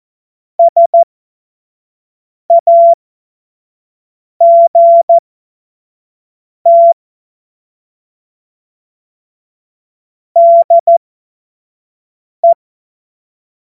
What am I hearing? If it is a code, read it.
Morse code, heard as a keyed tone, about 14 wpm, spaced as for 5 wpm, 680 Hz. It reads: SAGT DE
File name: Morsekurs / Witze / Witz 21. März 2023 25 F70